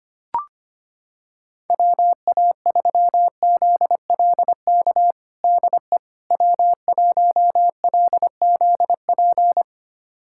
Morse code, heard as a keyed tone, about 25 wpm, 700 Hz. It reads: WA3ZLK DE W1LZP
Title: USA Callsign DE Callsign 25wpm